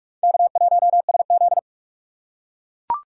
Morse code, heard as a keyed tone, about 43 wpm, 700 Hz. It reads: K1SZ
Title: USA Call Signs - 4 Characters - 2x Character Spacing - 45wpm